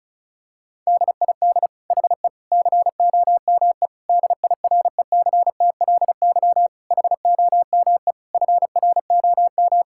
Morse code, heard as a keyed tone, about 36 wpm, 700 Hz. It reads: DID HE COME DIRECTLY HOME FROM
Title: Sentences from Top 1000 Words - Review Part 5 - 35wpm